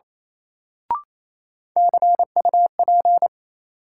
Morse code, heard as a keyed tone, about 28 wpm, 700 Hz. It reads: CUP
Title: Sentences from Top 1600 Words - New Word Drill Alt - 28wpm